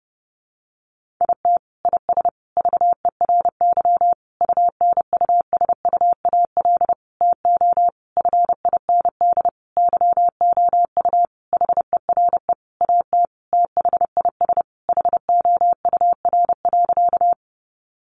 Morse code, heard as a keyed tone, about 30 wpm, 700 Hz. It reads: IT IS VERY UNUSUAL TO FIND YOU HERE AT THIS HOUR.
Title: Sentences from Top 1400 Words 30wpm